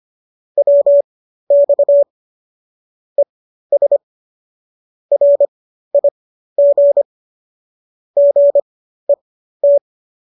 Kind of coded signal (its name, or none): Morse code